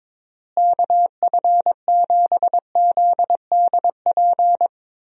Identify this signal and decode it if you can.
Morse code, heard as a keyed tone, about 22 wpm, 700 Hz. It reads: KF7ZDP
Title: USA Call Signs - 6 Characters - Repeated 3x - 22wpm